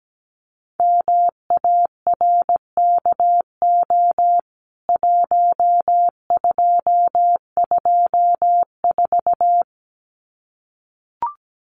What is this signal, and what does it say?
Morse code, heard as a keyed tone, about 17 wpm, 700 Hz. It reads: MARKO 1224
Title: CWT Exchange 17wpm